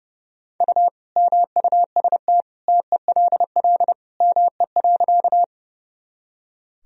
Morse code, heard as a keyed tone, about 30 wpm, 700 Hz. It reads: U MUST TELL ME.
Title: Sentences from Top 200 Words and Abbreviations 30wpm